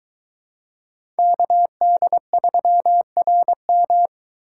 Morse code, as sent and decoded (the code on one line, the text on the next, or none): -.- -.. ...-- .-. --
KD3RM